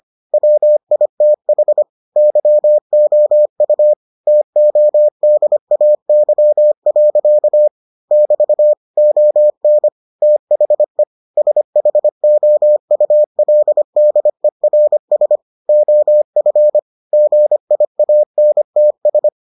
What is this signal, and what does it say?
Morse code, heard as a keyed tone, about 25 wpm, 600 Hz. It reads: WITH YOU TODAY. <BT> ON THE SHOULDERS OF GIANTS